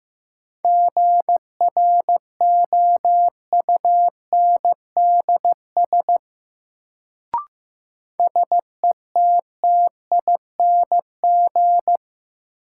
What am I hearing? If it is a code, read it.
Morse code, heard as a keyed tone, about 15 wpm, 700 Hz. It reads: GROUNDS SETTING
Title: Sentences from Top 1300 Words - New Word Drill Alt - 15wpm